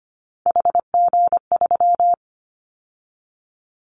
Morse code, heard as a keyed tone, about 25 wpm, 700 Hz. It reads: HG3
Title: Callsign Prefixes for Top 25 DXCC - 25wpm